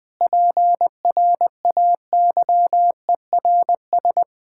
Morse code, heard as a keyed tone, about 20 wpm, 700 Hz. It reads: PRAYERS
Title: Sentences with 8-Word Phrases - 20wpm